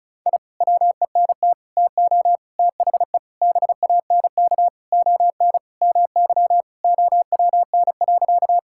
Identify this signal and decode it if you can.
Morse code, heard as a keyed tone, about 36 wpm, 700 Hz. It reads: I WENT TO THE BANK ON MY OWN.
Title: Sentences from Top 1000 Words - Review Part 6 - 35wpm